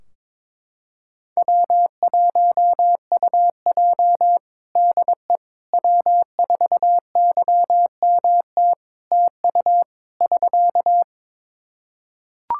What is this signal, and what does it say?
Morse code, heard as a keyed tone, about 22 wpm, 700 Hz. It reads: W1UJ DE W4YMT TU <SK>
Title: Practice QSO - Exchange 4 - Ending - 22wpm